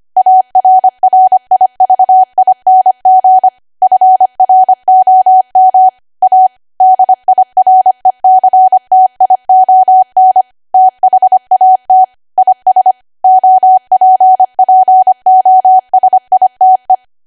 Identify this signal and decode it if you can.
Morse code, heard as a keyed tone, about 25 wpm, 750 Hz. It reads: ARRIVING FROM A DIRECTION THAT IS OPPOSITE
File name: Archive / 25 WPM / 25WPM